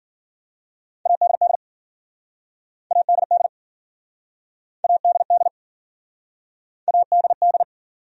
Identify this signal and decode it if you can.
Morse code, heard as a keyed tone, about 52 wpm, 700 Hz. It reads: ADD ADD ADD ADD